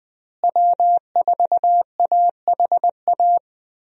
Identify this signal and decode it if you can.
Morse code, heard as a keyed tone, about 20 wpm, 700 Hz. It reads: W4AHA
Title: USA Call Signs - 5 Characters - Repeated 3x - 20wpm